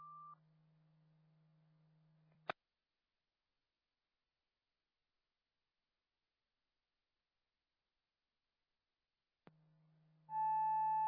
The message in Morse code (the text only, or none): none